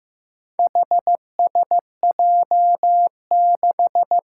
Morse code, heard as a keyed tone, about 15 wpm, 700 Hz. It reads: HSJ6
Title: Four Challenging Letter-Number Combinations 15wpm